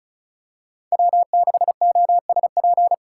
Morse code, heard as a keyed tone, about 34 wpm, 700 Hz. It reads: W6OSP
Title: K1USN SST QSO Practice - 35wpm